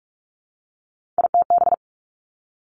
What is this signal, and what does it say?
Morse code, heard as a keyed tone, about 46 wpm, 700 Hz. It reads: ITB
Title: Three Challenging Letter-Number Combinations 45wpm